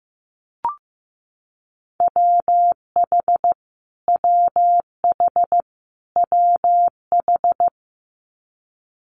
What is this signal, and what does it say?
Morse code, heard as a keyed tone, about 15 wpm, 700 Hz. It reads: WH WH WH
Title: Multi-letter Phonemes Repeated 3x 15wpm